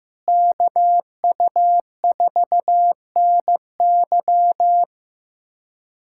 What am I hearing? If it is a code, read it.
Morse code, heard as a keyed tone, about 15 wpm, 700 Hz. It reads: KU4NY
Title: USA Callsign DE Callsign - 5 Characters - 15wpm